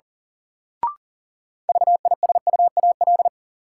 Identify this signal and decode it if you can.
Morse code, heard as a keyed tone, about 38 wpm, 700 Hz. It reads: VISUAL
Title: Sets of 5 Words Encore - New Word Drill - 40wpm